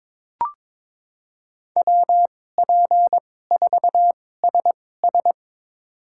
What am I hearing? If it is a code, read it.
Morse code, heard as a keyed tone, about 22 wpm, spaced as for 16 wpm, 700 Hz. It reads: WP4SS
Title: USA Call Signs - 5 Characters - 2x Character Spacing - 22wpm